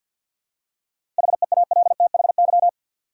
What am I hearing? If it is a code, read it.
Morse code, heard as a keyed tone, about 50 wpm, 700 Hz. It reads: HEALTHY
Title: Seven Letter Words - 50wpm